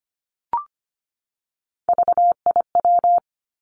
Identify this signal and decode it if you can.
Morse code, heard as a keyed tone, about 25 wpm, 700 Hz. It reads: VIW
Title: Three Challenging Letter-Number Combinations 25wpm